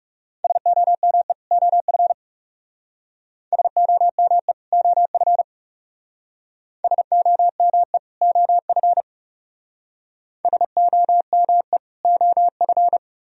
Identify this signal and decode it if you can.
Morse code, heard as a keyed tone, about 38 wpm, 700 Hz. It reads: SOME OF SOME OF SOME OF SOME OF